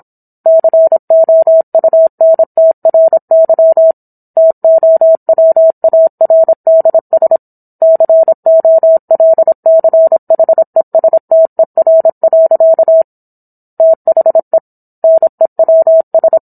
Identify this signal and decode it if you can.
Morse code, heard as a keyed tone, about 27 wpm, 650 Hz. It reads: COUNTRY TOWARDS COLCHESTER. THE NEWS